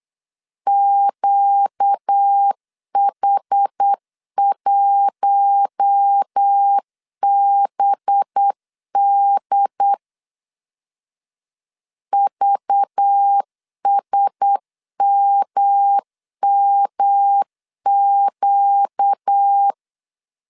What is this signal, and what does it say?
Morse code, heard as a keyed tone, about 8 wpm, 800 Hz. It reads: QH1BD VSMMQ
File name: CW / DET / 1 6 1500C S B 8wpm 5142